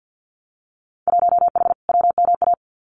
Morse code, heard as a keyed tone, about 50 wpm, 700 Hz. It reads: 1H WMA
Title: Winter Field Day Exchanges Encore 50wpm